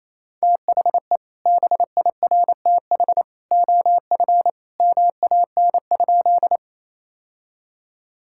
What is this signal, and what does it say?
Morse code, heard as a keyed tone, about 28 wpm, 700 Hz. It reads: THE BIRTH OF MAN?
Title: Sentences from Top 1000 Words - Review Part 5 - 28wpm